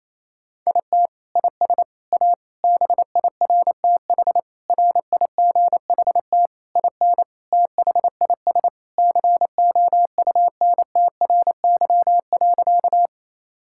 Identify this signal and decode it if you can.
Morse code, heard as a keyed tone, about 28 wpm, 700 Hz. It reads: IT IS A BIRTH RIGHT IN THIS COUNTRY.
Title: Sentences from Top 1000 Words - Review Part 5 - 28wpm